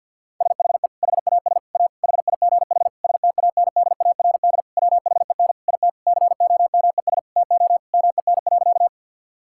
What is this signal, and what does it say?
Morse code, heard as a keyed tone, about 52 wpm, 700 Hz. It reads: SHE HAS A HIGH STANDARD WHEN IT COMES TO MEN.
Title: Sentences from Top 1000 Words - Review Part 6 - 50wpm